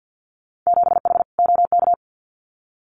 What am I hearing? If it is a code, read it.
Morse code, heard as a keyed tone, about 50 wpm, 700 Hz. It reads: 7H OK
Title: Winter Field Day Exchanges Encore 50wpm